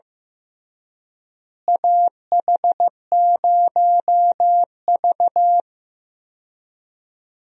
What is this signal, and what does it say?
Morse code, heard as a keyed tone, about 15 wpm, 700 Hz. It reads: AH0V
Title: USA Call Signs - 4 Characters - Repeated 3x - 15wpm